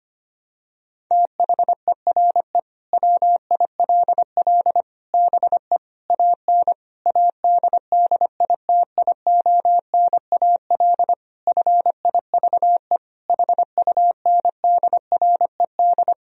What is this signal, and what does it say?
Morse code, heard as a keyed tone, about 25 wpm, 700 Hz. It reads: THERE WILL BE AN ADDITIONAL FIVE HUNDRED
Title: Sentences from Top 1000 Words - Review Part 4 - 25wpm